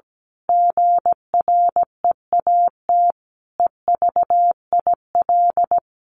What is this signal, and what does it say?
Morse code, heard as a keyed tone, about 17 wpm, 700 Hz. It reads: GREAT EVIL
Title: A Sound Of Thunder 17wpm